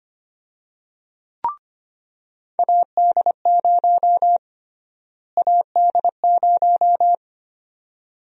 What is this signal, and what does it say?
Morse code, heard as a keyed tone, about 25 wpm, 700 Hz. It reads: AD0 AD0
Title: USA Call Sign Prefixes - Repeated 2x - 25wpm